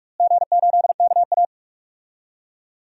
Morse code, heard as a keyed tone, about 46 wpm, 700 Hz. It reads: G8KA
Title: DXCC-100 Call Signs - 4 Characters - Repeated 3x - 45wpm